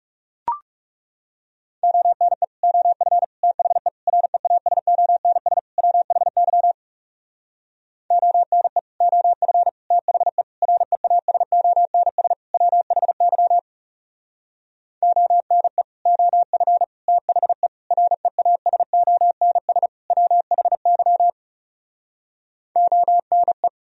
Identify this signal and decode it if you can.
Morse code, heard as a keyed tone, about 39 wpm, 700 Hz. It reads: ONE OF THE REASONS WHY ONE OF THE REASONS WHY ONE OF THE REASONS WHY ONE